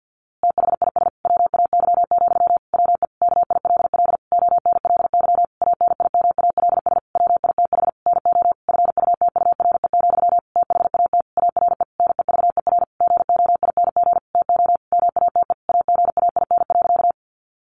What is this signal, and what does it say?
Morse code, heard as a keyed tone, about 50 wpm, 700 Hz. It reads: THIS WAY, WE KILL ONLY ANIMALS WITH NO FUTURE, THAT ARE NEVER GOING TO MATE AGAIN.
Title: A Sound Of Thunder 50wpm